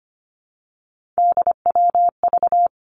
Morse code, heard as a keyed tone, about 25 wpm, 700 Hz. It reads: DWV
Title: Three Challenging Letter-Number Combinations 25wpm